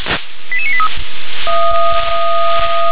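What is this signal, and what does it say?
Five-tone selective call (ZVEI), heard as quick runs of five tones, tones of about 70 ms. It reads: 88983
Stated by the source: ZVEI five-tone selective call received by SDR.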